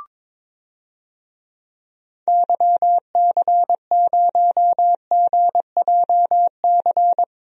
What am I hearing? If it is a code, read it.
Morse code, heard as a keyed tone, about 22 wpm, 700 Hz. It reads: YC0GJC
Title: DXCC-100 Call Signs - 6 Characters - Repeated 3x - 22wpm